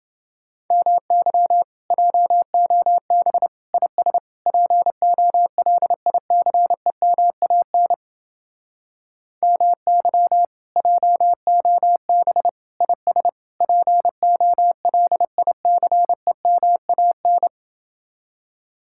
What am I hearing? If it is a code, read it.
Morse code, heard as a keyed tone, about 30 wpm, 700 Hz. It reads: MY JOB IS POLICEMAN MY JOB IS POLICEMAN